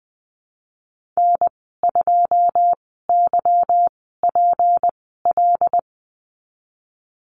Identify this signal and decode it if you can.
Morse code, heard as a keyed tone, about 20 wpm, spaced as for 14 wpm, 700 Hz. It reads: N2YPL
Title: 161 Lesson 26g - Letter g - Call Signs 20wpm